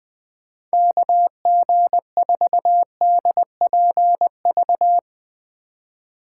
Morse code, heard as a keyed tone, about 20 wpm, 700 Hz. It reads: KG4DPV
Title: USA Call Signs - 6x Character Spacing - 20wpm